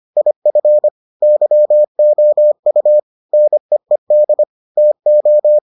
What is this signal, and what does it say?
Morse code, heard as a keyed tone, about 25 wpm, 600 Hz. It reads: IF YOU NEED TO